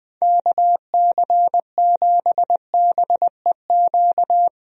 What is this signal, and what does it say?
Morse code, heard as a keyed tone, about 20 wpm, 700 Hz. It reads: KC7BEQ
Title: USA Call Signs - 6 Characters - Repeated 2x - 20wpm